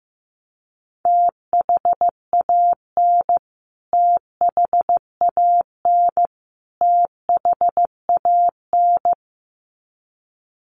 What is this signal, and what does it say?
Morse code, heard as a keyed tone, about 15 wpm, 700 Hz. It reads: THAN THAN THAN
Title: Top 200 Words Repeated 15wpm